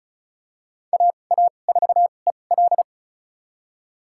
Morse code, heard as a keyed tone, about 34 wpm, spaced as for 25 wpm, 700 Hz. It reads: AA4EL
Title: USA Call Signs - 5 Characters - 2x Character Spacing - 35wpm